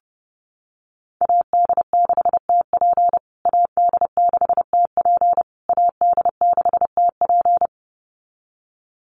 Morse code, heard as a keyed tone, about 28 wpm, 700 Hz. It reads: AD6TP AD6TP AD6TP
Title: USA Call Signs Repeated 3x 30wpm